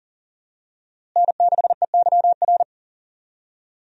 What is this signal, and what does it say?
Morse code, heard as a keyed tone, about 39 wpm, 700 Hz. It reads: N6EYR
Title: USA Call Signs - 5 Characters - Repeated 3x - 40wpm